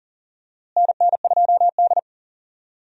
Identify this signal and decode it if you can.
Morse code, heard as a keyed tone, about 40 wpm, 700 Hz. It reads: NN2D
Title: USA Call Signs - 4 Characters - Repeated 3x - 40wpm